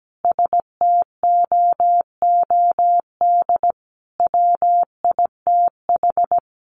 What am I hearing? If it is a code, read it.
Morse code, heard as a keyed tone, about 17 wpm, 700 Hz. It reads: STOOD WITH
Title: A Sound Of Thunder 17wpm